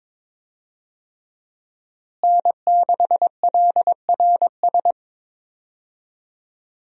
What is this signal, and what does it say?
Morse code, heard as a keyed tone, about 22 wpm, 700 Hz. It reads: N6LRS
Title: USA Call Signs - 5 Characters - Repeated 3x - 22wpm